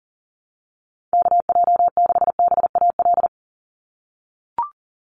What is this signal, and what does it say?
Morse code, heard as a keyed tone, about 40 wpm, 690 Hz. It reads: KJ6BAL
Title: USA Call Signs 40wpm